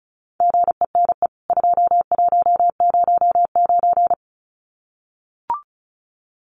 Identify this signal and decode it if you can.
Morse code, heard as a keyed tone, about 34 wpm, 700 Hz. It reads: GENE 2109
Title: CWT Exchange 35wpm